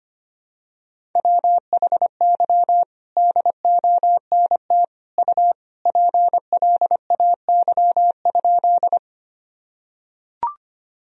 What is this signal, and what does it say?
Morse code, heard as a keyed tone, about 25 wpm, 700 Hz. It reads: WHY DONT U PLAY?
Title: Sentences from Top 300 Words and Abbreviations 25wpm